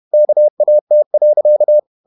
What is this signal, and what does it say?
Morse code, heard as a keyed tone, about 31 wpm, 600 Hz. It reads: KAT.
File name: Ferie QRQ 31wpm